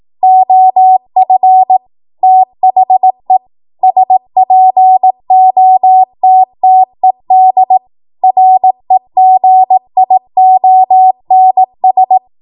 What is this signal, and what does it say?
Morse code, heard as a keyed tone, about 18 wpm, 750 Hz. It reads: OF THE SPOTTED REGIONS
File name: Morse / 18WPM